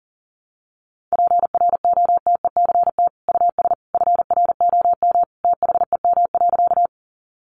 Morse code, heard as a keyed tone, about 40 wpm, 700 Hz. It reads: PROTECT US FROM THEM.
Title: Sentences from Top 900 Words 40wpm